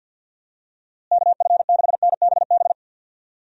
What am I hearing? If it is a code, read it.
Morse code, heard as a keyed tone, about 48 wpm, 700 Hz. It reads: KF6NBB
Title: USA Call Signs - 6 Characters - Repeated 3x - 50wpm